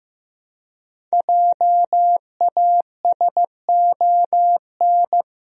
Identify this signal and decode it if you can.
Morse code, heard as a keyed tone, about 15 wpm, 700 Hz. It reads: JASON
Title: USA Common Names - Max 5 Letters - 15wpm